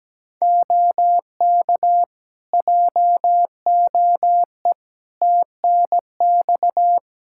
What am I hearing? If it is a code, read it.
Morse code, heard as a keyed tone, about 17 wpm, 700 Hz. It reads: OK JOE TNX
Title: K1USN SST QSO Practice 17wpm